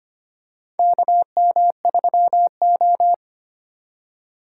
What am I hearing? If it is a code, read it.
Morse code, heard as a keyed tone, about 25 wpm, 700 Hz. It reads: KM3O
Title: USA Call Signs - 8x Character Spacing - 25wpm